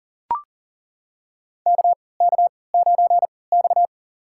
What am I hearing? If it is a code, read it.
Morse code, heard as a keyed tone, about 37 wpm, spaced as for 22 wpm, 700 Hz. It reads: KK9X
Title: USA Call Signs - 4 Characters - 3x Character Spacing - 40wpm